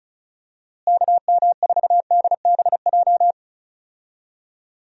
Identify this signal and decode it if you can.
Morse code, heard as a keyed tone, about 34 wpm, 700 Hz. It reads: KM4DBJ
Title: USA Call Signs - 6 Characters - Repeated 2x - 35wpm